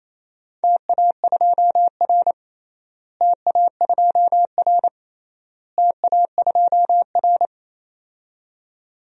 Morse code, heard as a keyed tone, about 28 wpm, 700 Hz. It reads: TA2R TA2R TA2R
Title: DXCC-100 Call Signs - 4 Characters - Repeated 3x - 28wpm